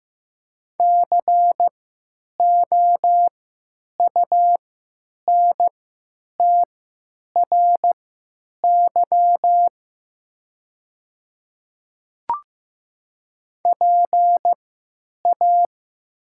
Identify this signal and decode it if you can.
Morse code, heard as a keyed tone, about 15 wpm, spaced as for 9 wpm, 700 Hz. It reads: COUNTRY PA
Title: 3x Character Spacing - Top 300 Words - 15wpm